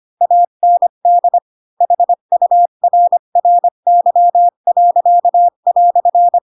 Morse code, heard as a keyed tone, about 25 wpm, 700 Hz. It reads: AND HURRY."
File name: Pinocchio / Pinocchio ch4 20 25 ik1hge